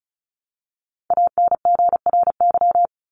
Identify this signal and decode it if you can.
Morse code, heard as a keyed tone, about 34 wpm, 700 Hz. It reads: ANGRY
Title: Top 1751-2000 Words 35wpm